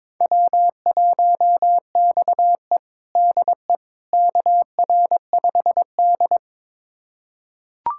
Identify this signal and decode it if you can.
Morse code, heard as a keyed tone, about 22 wpm, 700 Hz. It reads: W1XE DE KR5D
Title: USA Callsign DE Callsign - 4 Characters - 22wpm